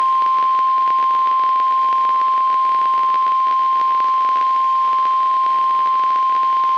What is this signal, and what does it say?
Emergency Alert System (SAME) tones (the 1050 Hz attention tone). It is heard from 0.4 to 6.0 s.
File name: eas messages / EAS WXM95 Towanda